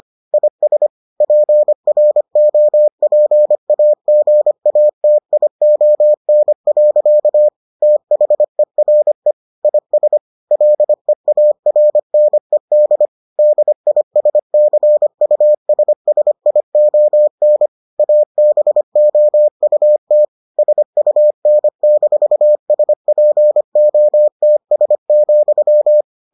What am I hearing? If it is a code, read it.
Morse code, heard as a keyed tone, about 25 wpm, 600 Hz. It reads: IS PROPAGATION. THERE IS LEARNED DISCUSSION ABOUT SUN-SPOTS,